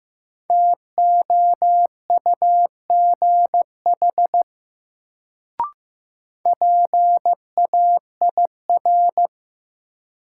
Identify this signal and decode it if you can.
Morse code, heard as a keyed tone, about 15 wpm, 700 Hz. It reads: TOUGH PAIR
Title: Sentences from Top 1200 Words - New Word Drill - 15wpm